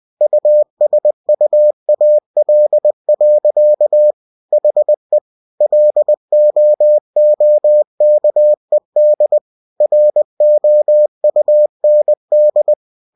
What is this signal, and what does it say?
Morse code, heard as a keyed tone, about 20 wpm, 600 Hz. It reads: USUAL. HE LOOKED ROUND
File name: cw-books / winnie-the-pooh / winnie-20wpm-page-0083